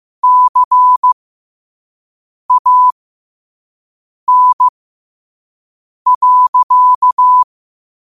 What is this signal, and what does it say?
Morse code, heard as a keyed tone, about 15 wpm, spaced as for 5 wpm, 1000 Hz. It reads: CAN.